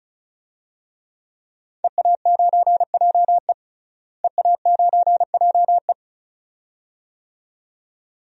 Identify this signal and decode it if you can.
Morse code, heard as a keyed tone, about 34 wpm, 700 Hz. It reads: EA9JE EA9JE
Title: DXCC-100 Call Signs - 5 Characters - Repeated 2x - 35wpm